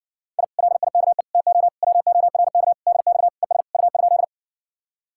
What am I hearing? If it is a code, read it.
Morse code, heard as a keyed tone, about 59 wpm, 700 Hz. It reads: I LIKE TO WORK DX ES U?